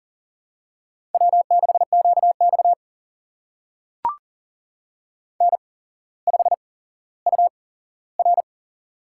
Morse code, heard as a keyed tone, about 40 wpm, spaced as for 17 wpm, 700 Hz. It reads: W6QX N5UR
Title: USA Call Signs - 4 Characters - 8x Character Spacing - 40wpm